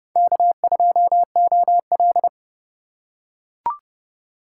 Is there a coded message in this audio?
Yes: Morse code from 0.2 to 2.3 s, heard as a keyed tone, about 29 wpm, 700 Hz. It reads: K2OL